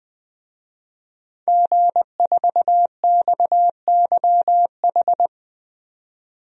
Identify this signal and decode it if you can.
Morse code, heard as a keyed tone, about 20 wpm, 700 Hz. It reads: G4XYH
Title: DXCC-100 Call Signs - 5 Characters - 20wpm